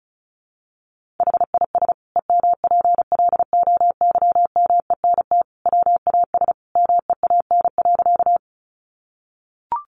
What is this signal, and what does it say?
Morse code, heard as a keyed tone, about 34 wpm, 700 Hz. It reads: HIS EMPLOYMENT WAS MEAN.
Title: Sentences from Top 1500 Words 35wpm